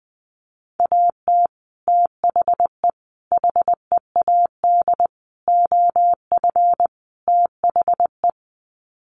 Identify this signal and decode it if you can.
Morse code, heard as a keyed tone, about 19 wpm, 700 Hz. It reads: AT THE HEAD OF THE
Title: Sets of 5 Words 20wpm